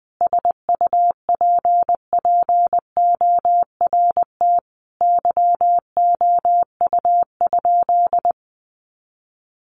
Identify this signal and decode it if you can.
Morse code, heard as a keyed tone, about 20 wpm, 700 Hz. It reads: SUPPORT YOU?
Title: Sentences from Top 800 Words 20wpm